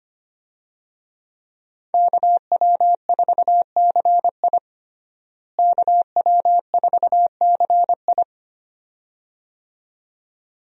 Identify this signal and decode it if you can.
Morse code, heard as a keyed tone, about 25 wpm, 700 Hz. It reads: KW4CI KW4CI
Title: USA Call Signs - 5 Characters - Repeated 2x - 25wpm